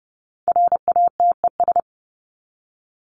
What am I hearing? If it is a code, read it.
Morse code, heard as a keyed tone, about 30 wpm, 700 Hz. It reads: RATES
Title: ES:S Suffix - Top 100 Words with Base Words 30wpm